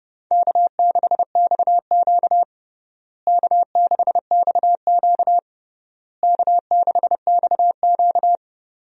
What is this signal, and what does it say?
Morse code, heard as a keyed tone, about 32 wpm, 700 Hz. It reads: K6XQ K6XQ K6XQ
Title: USA Call Signs - 4 Characters - Repeated 3x - 30wpm